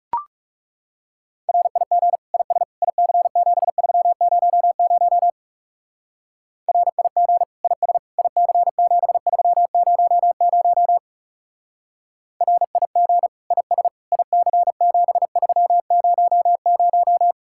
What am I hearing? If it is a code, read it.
Morse code, heard as a keyed tone, about 37 wpm, 700 Hz. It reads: RIG IS IC7300 RIG IS IC7300 RIG IS IC7300
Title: Speed-Racing Common QSO Phrases Encore - 30wpm